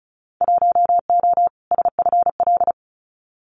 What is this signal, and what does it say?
Morse code, heard as a keyed tone, about 36 wpm, 700 Hz. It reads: 1O SFL
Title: Winter Field Day Exchanges 35wpm